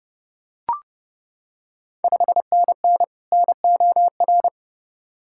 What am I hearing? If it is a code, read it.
Morse code, heard as a keyed tone, about 32 wpm, 700 Hz. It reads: 5NN NOR
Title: Ontario QSO Party Exchange 30wpm